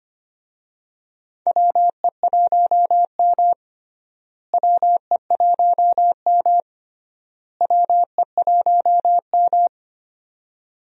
Morse code, heard as a keyed tone, about 25 wpm, 700 Hz. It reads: WE1M WE1M WE1M
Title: USA Call Signs - 4 Characters - Repeated 3x - 25wpm